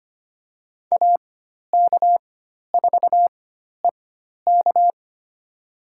Morse code, heard as a keyed tone, about 25 wpm, spaced as for 12 wpm, 700 Hz. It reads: AK4EK
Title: USA Call Signs - 5 Characters - 4x Character Spacing - 25wpm